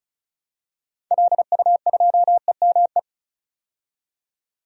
Morse code, heard as a keyed tone, about 34 wpm, 700 Hz. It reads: LU2EME
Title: DXCC-100 Call Signs - 6 Characters - 35wpm